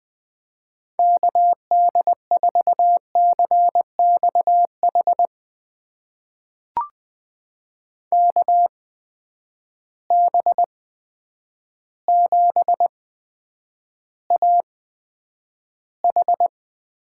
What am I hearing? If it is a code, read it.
Morse code, heard as a keyed tone, about 20 wpm, spaced as for 9 wpm, 700 Hz. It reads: KD4CXH KB7AH
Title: USA Call Signs - 6 Characters - 8x Character Spacing - 20wpm